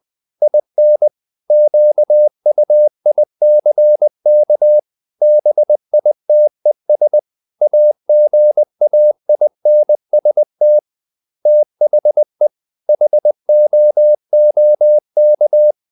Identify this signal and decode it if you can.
Morse code, heard as a keyed tone, about 20 wpm, 600 Hz. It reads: IN QUICK BITES AGAINST THE HOOK
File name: the old man and the sea part 2 20wpm